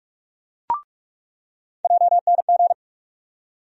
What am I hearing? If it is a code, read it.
Morse code, heard as a keyed tone, about 45 wpm, 700 Hz. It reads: JNG
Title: USA Call Sign Postfixes - 3 Letters 45wpm